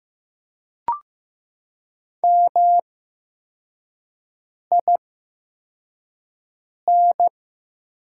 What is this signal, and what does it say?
Morse code, heard as a keyed tone, about 15 wpm, spaced as for 4 wpm, 700 Hz. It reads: MIN
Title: 8x Character Spacing - Top 500 Words - 15wpm